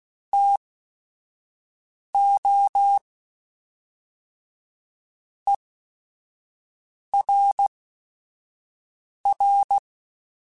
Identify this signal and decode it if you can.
Morse code, heard as a keyed tone, about 16 wpm, spaced as for 5 wpm, 800 Hz. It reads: TO ERR
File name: Chuckles / Chuckles-mp3s / HumanErr @5